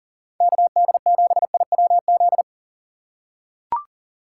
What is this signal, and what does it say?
Morse code, heard as a keyed tone, about 38 wpm, 700 Hz. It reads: KD7IWZ
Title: USA Call Signs - 6 Characters - 3x Character Spacing - 40wpm